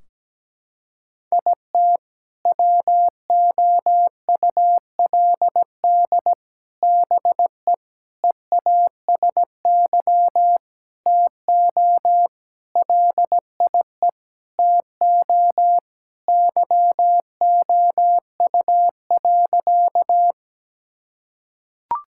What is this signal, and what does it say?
Morse code, heard as a keyed tone, about 17 wpm, 700 Hz. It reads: IT WOULD BE EASY TO LIE TO YOU.
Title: Sentences from Top 1600 Words 17wpm